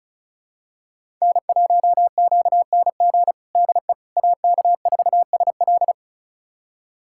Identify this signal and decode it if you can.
Morse code, heard as a keyed tone, about 36 wpm, 700 Hz. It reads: N1QNG DE AK4SL
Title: USA Callsign DE Callsign - 5 Characters - 35wpm